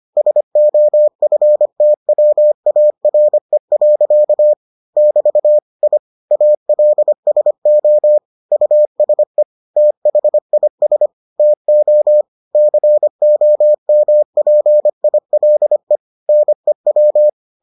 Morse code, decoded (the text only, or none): SOFTWARE. <BT> I ALSO USE THIS TO COMPILE NEW